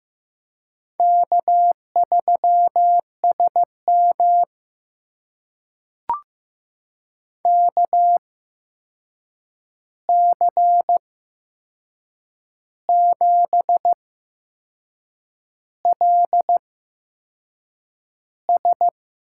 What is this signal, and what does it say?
Morse code, heard as a keyed tone, about 15 wpm, spaced as for 6 wpm, 700 Hz. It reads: K3SM KC7LS
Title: USA Call Signs - 8x Character Spacing - 15wpm